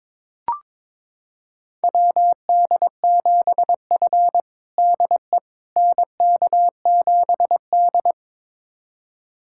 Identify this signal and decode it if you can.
Morse code, heard as a keyed tone, about 22 wpm, 700 Hz. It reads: WD7F DE NK7D
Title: USA Callsign DE Callsign - 4 Characters - 22wpm